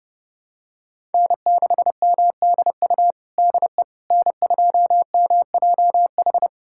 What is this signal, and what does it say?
Morse code, heard as a keyed tone, about 32 wpm, 700 Hz. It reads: N6MDU DE N2MJH
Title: USA Callsign DE Callsign - 5 Characters - 30wpm